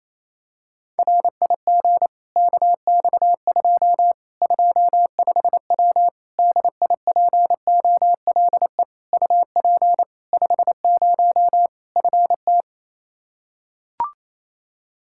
Morse code, heard as a keyed tone, about 28 wpm, 700 Hz. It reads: RIG KX2 25W DIPOLE UP 50 FT
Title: Practice QSO - Exchange 2 - Info - 28wpm